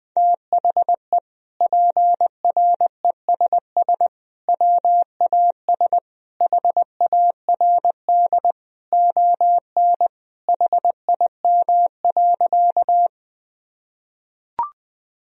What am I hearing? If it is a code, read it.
Morse code, heard as a keyed tone, about 20 wpm, 700 Hz. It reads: THE PRESS WAS HARD ON HIM.
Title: Sentences from Top 1000 Words - Review Part 1 - 20wpm